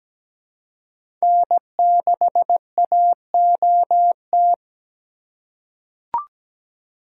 Morse code, heard as a keyed tone, about 17 wpm, 700 Hz. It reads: N6AOT
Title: USA Call Signs - 2x Character Spacing - 17wpm